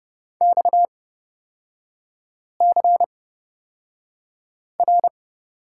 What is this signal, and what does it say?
Morse code, heard as a keyed tone, about 30 wpm, spaced as for 5 wpm, 700 Hz. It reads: XCR
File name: Lesson 38e - Letter q - Three Characters 30wpm